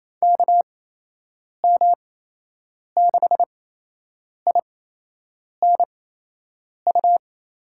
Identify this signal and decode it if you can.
Morse code, heard as a keyed tone, about 28 wpm, spaced as for 8 wpm, 700 Hz. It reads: KM6INU
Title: USA Call Signs - 8x Character Spacing - 28wpm